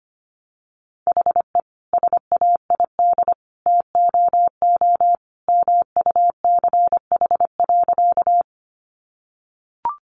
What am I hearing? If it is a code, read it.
Morse code, heard as a keyed tone, about 25 wpm, 700 Hz. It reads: HE SAID TOO MUCH.
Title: Sentences from Top 200 Words 25wpm